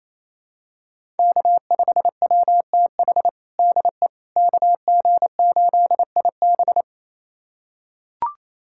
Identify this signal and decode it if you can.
Morse code, heard as a keyed tone, about 28 wpm, 700 Hz. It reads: K5WTH DE KG8IB
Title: USA Callsign DE Callsign - 5 Characters - 28wpm